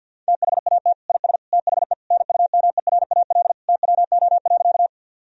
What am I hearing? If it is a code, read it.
Morse code, heard as a keyed tone, about 52 wpm, 700 Hz. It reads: THAT IS THE NUMERAL TWO.
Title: Sentences from Top 1000 Words - Review Part 2 - 50wpm